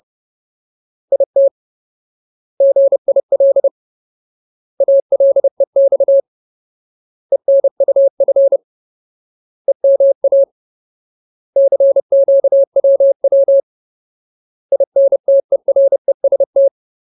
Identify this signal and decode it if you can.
Morse code, heard as a keyed tone, about 30 wpm, 550 Hz. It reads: IT GIL ALEX ENUF EMA CQWW INTEREST
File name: Ultimate Word List / 30WPM Ultimate